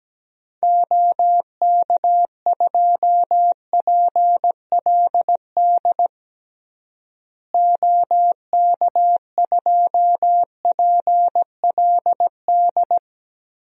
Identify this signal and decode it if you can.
Morse code, heard as a keyed tone, about 17 wpm, 700 Hz. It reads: OK2PLD OK2PLD
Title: DXCC-100 Call Signs - 6 Characters - Repeated 3x - 17wpm